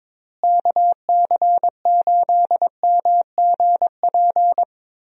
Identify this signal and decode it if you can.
Morse code, heard as a keyed tone, about 22 wpm, 700 Hz. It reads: KC8MGP
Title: USA Call Signs - 6 Characters - Repeated 2x - 22wpm